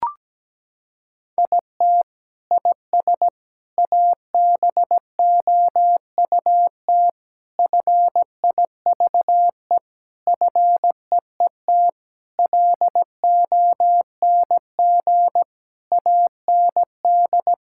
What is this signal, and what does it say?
Morse code, heard as a keyed tone, about 17 wpm, 700 Hz. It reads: IT IS ABOUT FIVE FEET LONG AND
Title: Sentences from Top 1000 Words - Review Part 6 - 17wpm